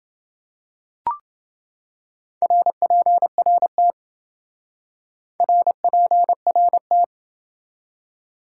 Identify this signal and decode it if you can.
Morse code, heard as a keyed tone, about 28 wpm, 700 Hz. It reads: RPRT RPRT